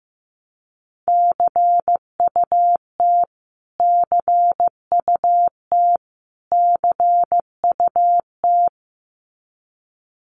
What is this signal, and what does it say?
Morse code, heard as a keyed tone, about 15 wpm, 700 Hz. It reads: CUT CUT CUT
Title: Top 500 Words Repeated 15wpm